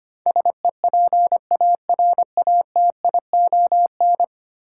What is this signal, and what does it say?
Morse code, heard as a keyed tone, about 25 wpm, 700 Hz. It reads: SEPARATION
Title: Sets of 5 Words - New Word Drill - 25wpm